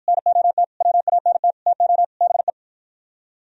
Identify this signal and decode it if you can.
Morse code, heard as a keyed tone, about 49 wpm, 700 Hz. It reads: NOT WANT TO BE